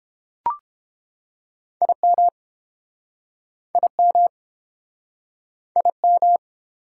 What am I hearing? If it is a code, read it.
Morse code, heard as a keyed tone, about 30 wpm, 700 Hz. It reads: IM IM IM